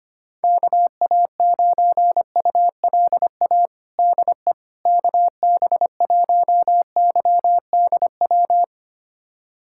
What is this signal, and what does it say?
Morse code, heard as a keyed tone, about 25 wpm, 700 Hz. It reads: KA9ULA DE KB1YDW
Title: USA Callsign DE Callsign - 6 Characters - 25wpm